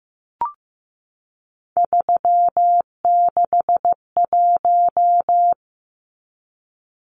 Morse code, heard as a keyed tone, about 15 wpm, 700 Hz. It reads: 361
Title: Numbers - 3-Digit 15wpm